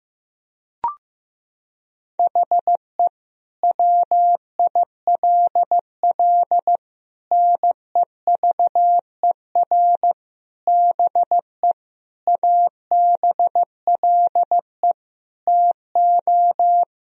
Morse code, heard as a keyed tone, about 15 wpm, 700 Hz. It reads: HE WILL NEVER BE ABLE TO
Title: Sentences from Top 1000 Words - Review Part 2 - 15wpm